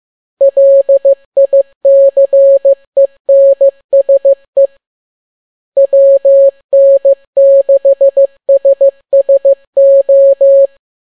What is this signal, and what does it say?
Morse code, heard as a keyed tone, about 15 wpm, 550 Hz. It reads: LICENSE WN6SSO
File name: n6tt p1 15C13